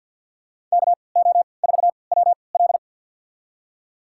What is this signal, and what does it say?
Morse code, heard as a keyed tone, about 47 wpm, spaced as for 28 wpm, 700 Hz. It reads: KO4WL
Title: USA Call Signs - 5 Characters - 3x Character Spacing - 50wpm